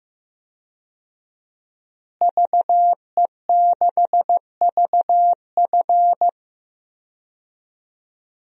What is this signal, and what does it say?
Morse code, heard as a keyed tone, about 15 wpm, 700 Hz. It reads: VE6VF
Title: DXCC-100 Call Signs - 5 Characters - Repeated 2x - 15wpm